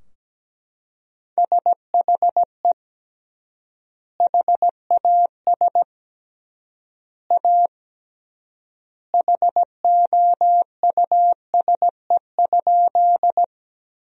Morse code, heard as a keyed tone, about 17 wpm, 700 Hz. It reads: SHE HAS A HOUSE?
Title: 3x Word-Spacing - Sentences from Top 200 Words - 17wpm